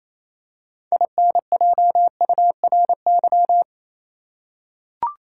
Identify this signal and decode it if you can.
Morse code, heard as a keyed tone, about 28 wpm, 700 Hz. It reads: INJURY
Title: Six Letter Words - 28wpm